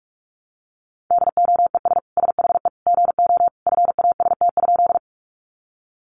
Morse code, heard as a keyed tone, about 46 wpm, 700 Hz. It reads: DOES SHE GO FAST?
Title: Sentences from Top 400 Words 45wpm